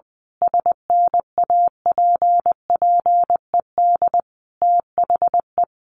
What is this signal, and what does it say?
Morse code, heard as a keyed tone, about 20 wpm, 700 Hz. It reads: SNAPPED THE
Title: A Sound Of Thunder 20wpm